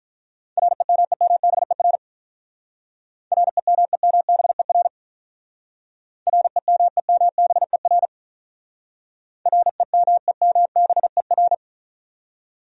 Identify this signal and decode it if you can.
Morse code, heard as a keyed tone, about 42 wpm, 700 Hz. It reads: REMEMBER REMEMBER REMEMBER REMEMBER
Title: Speed-Racing Top 500 Words 35wpm